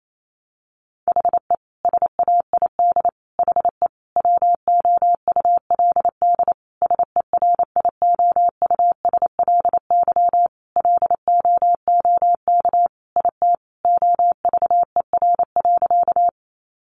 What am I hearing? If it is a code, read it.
Morse code, heard as a keyed tone, about 28 wpm, 700 Hz. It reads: HE SAID HE WOULD SERIOUSLY LOOK IT OVER.
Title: Sentences from Top 1200 Words 28wpm